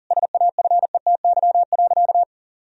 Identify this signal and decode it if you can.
Morse code, heard as a keyed tone, about 40 wpm, 700 Hz. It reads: SAFETY.
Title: Sentences from Top 1000 Words - Review Part 4 - 40wpm